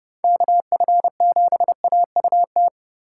Morse code, heard as a keyed tone, about 39 wpm, 700 Hz. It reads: KF7AUT